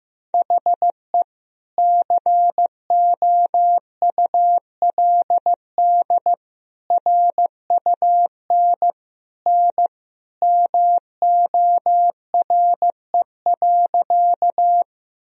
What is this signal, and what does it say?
Morse code, heard as a keyed tone, about 15 wpm, 700 Hz. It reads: HE COULD RUN N MORE.
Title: Sentences from Top 300 Words and Abbreviations 15wpm